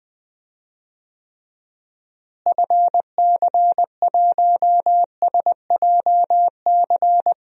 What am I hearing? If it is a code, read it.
Morse code, heard as a keyed tone, about 20 wpm, 700 Hz. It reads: FC1SJC
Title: DXCC-100 Call Signs - 6 Characters - Repeated 3x - 20wpm